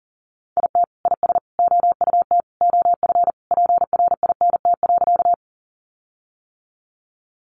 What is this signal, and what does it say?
Morse code, heard as a keyed tone, about 40 wpm, 700 Hz. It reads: IT IS OUT OF PRINT.
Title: Sentences from Top 1400 Words 40wpm